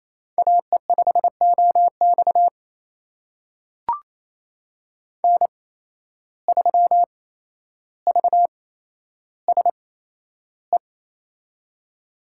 Morse code, heard as a keyed tone, about 27 wpm, spaced as for 12 wpm, 700 Hz. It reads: AE5OX N3VSE
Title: USA Call Signs - 8x Character Spacing - 28wpm